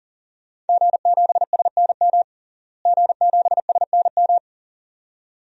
Morse code, heard as a keyed tone, about 40 wpm, 700 Hz. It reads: G7SNM G7SNM
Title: DXCC-100 Call Signs - 5 Characters - Repeated 2x - 40wpm